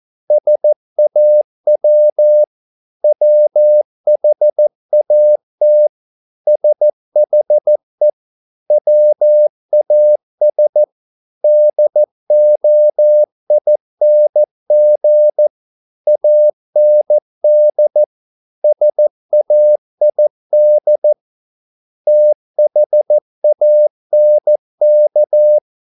Morse code, heard as a keyed tone, about 14 wpm, 600 Hz. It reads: SAW WHAT SHE WAS DOING AND SAID THANK